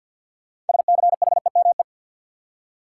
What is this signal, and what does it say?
Morse code, heard as a keyed tone, about 52 wpm, 700 Hz. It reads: SCHEME
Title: Six Letter Words - 50wpm